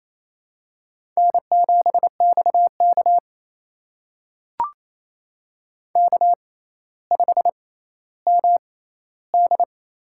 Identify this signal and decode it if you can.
Morse code, heard as a keyed tone, about 28 wpm, spaced as for 14 wpm, 700 Hz. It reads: N7XK K5MD
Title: USA Call Signs - 4 Characters - 6x Character Spacing - 28wpm